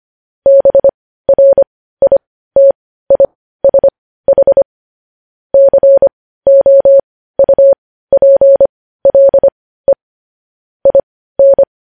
Morse code, heard as a keyed tone, about 25 wpm, spaced as for 15 wpm, 550 Hz. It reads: BRITISH COUPLE IN
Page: news headlines in morse code at 15 wpm